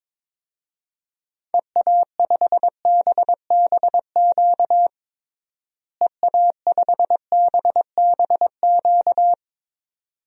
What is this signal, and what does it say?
Morse code, heard as a keyed tone, about 22 wpm, 700 Hz. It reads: EA5BBQ EA5BBQ
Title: DXCC-100 Call Signs - 6 Characters - Repeated 2x - 22wpm